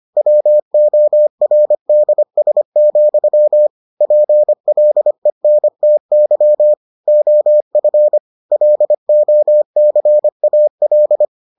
Morse code, heard as a keyed tone, about 25 wpm, 600 Hz. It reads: WORDS, PLENTY OF LOCAL